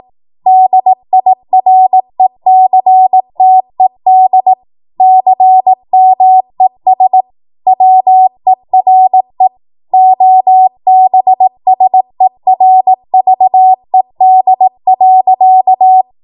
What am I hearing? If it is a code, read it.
Morse code, heard as a keyed tone, about 18 wpm, 750 Hz. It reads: DIRECTED CMES WERE OBSERVED.
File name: Morse / 18WPM